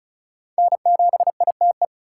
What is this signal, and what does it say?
Morse code, heard as a keyed tone, about 34 wpm, 700 Hz. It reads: N7ITE
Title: USA Call Signs - 5 Characters - 4x Character Spacing - 35wpm